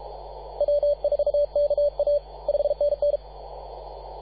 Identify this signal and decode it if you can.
Morse code, heard as a keyed tone, about 33 wpm, 580 Hz. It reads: W4KA 5NN
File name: posts / 2015-sept-24-2118z-t2gc-18-071-cw